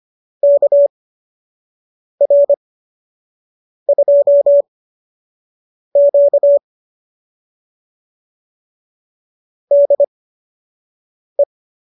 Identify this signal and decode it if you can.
Morse code, heard as a keyed tone, about 25 wpm, spaced as for 6 wpm, 590 Hz. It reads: KR2Q DE